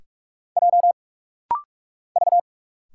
Morse code, heard as a keyed tone, about 46 wpm, 700 Hz. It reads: J V
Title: Single Letters - Rapid-Fire 45wpm